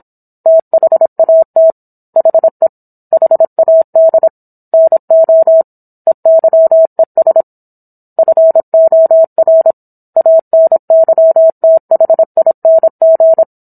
Morse code, heard as a keyed tone, about 26 wpm, 650 Hz. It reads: THAT HE HAD NO EYES FOR ANYTHING